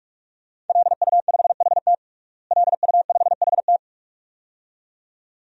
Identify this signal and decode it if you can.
Morse code, heard as a keyed tone, about 46 wpm, 700 Hz. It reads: LU5HT LU5HT
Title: DXCC-100 Call Signs - 5 Characters - Repeated 2x - 45wpm